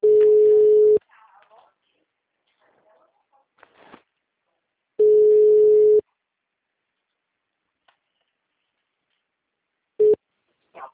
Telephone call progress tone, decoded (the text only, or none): ringback tone